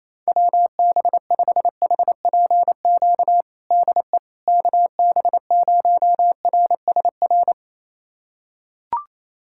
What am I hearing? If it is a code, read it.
Morse code, heard as a keyed tone, about 28 wpm, 700 Hz. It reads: WB5HPQ DE KB0RSR
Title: USA Callsign DE Callsign - 6 Characters - 28wpm